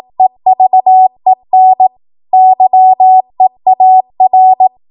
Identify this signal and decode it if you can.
Morse code, heard as a keyed tone, about 18 wpm, 750 Hz. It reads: EVEN YEAR
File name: Morse / 18WPM